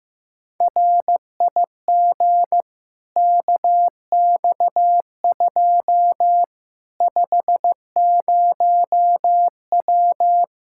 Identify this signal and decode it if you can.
Morse code, heard as a keyed tone, about 15 wpm, 700 Hz. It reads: RIG KX2 50W
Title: Practice QSO - Exchange 2 - Info - 15wpm